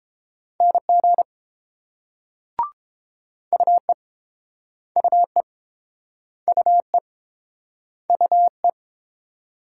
Morse code, heard as a keyed tone, about 30 wpm, 700 Hz. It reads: NG UE UE UE UE